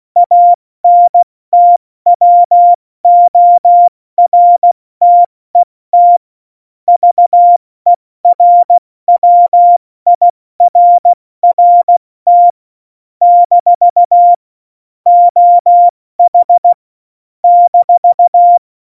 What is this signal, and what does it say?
Morse code, heard as a keyed tone, about 16 wpm, 700 Hz. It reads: ANTWORTET VERWIRRT - OH -
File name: Morsekurs / Witze / Witz 29. Oktober 2019, 70 F80